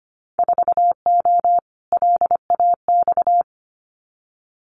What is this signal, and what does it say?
Morse code, heard as a keyed tone, about 25 wpm, 700 Hz. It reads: 4O LAX
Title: Winter Field Day Exchanges Encore 25wpm